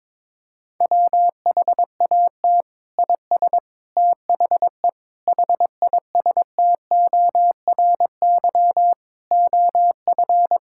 Morse code, heard as a keyed tone, about 22 wpm, 700 Hz. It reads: WHAT IS THE HISTORY OF
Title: Sentences from Top 1000 Words - Review Part 5 - 22wpm